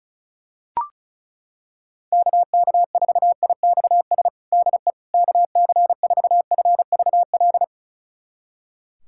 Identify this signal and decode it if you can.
Morse code, heard as a keyed tone, about 36 wpm, 700 Hz. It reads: KK4IXS DE KC4FVL
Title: USA Callsign DE Callsign - 6 Characters - 35wpm